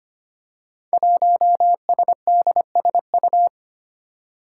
Morse code, heard as a keyed tone, about 25 wpm, 700 Hz. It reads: 1SDSU
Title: Five Challenging Letter-Number Combinations 25wpm